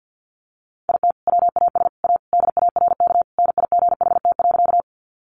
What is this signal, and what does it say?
Morse code, heard as a keyed tone, about 50 wpm, 700 Hz. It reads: IT WAS A DARK NIGHT.
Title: Sentences from Top 500 Words 50wpm